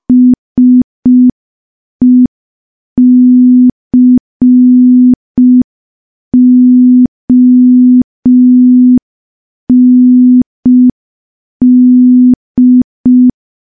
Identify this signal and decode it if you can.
Morse code, heard as a keyed tone, about 5 wpm, 250 Hz. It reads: SECOND
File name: morse / morse-training-05WPM